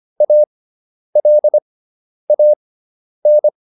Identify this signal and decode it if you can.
Morse code, heard as a keyed tone, about 25 wpm, spaced as for 10 wpm, 600 Hz. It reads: ALAN